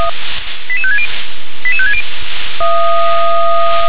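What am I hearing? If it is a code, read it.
Five-tone selective call (ZVEI), heard as quick runs of five tones, tones of about 70 ms. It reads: 88560 88560
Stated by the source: ZVEI five-tone call 88766 received by SDR.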